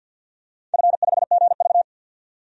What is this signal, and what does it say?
Morse code, heard as a keyed tone, about 49 wpm, 700 Hz. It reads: F5GV
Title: DXCC-100 Call Signs - 4 Characters - 50wpm